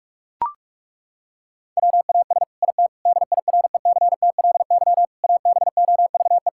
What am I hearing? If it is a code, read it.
Morse code, heard as a keyed tone, about 41 wpm, 700 Hz. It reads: WAS IT DIRECTLY ABOVE
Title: Sentences from Top 1000 Words - Review Part 5 - 45wpm